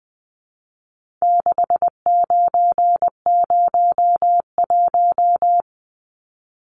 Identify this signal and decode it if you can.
Morse code, heard as a keyed tone, about 20 wpm, 700 Hz. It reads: 6901
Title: Numbers - 4-Digit 20wpm